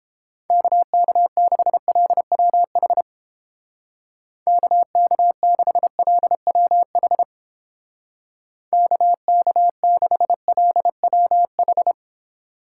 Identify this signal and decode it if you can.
Morse code, heard as a keyed tone, about 30 wpm, 700 Hz. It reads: KK6LWH KK6LWH KK6LWH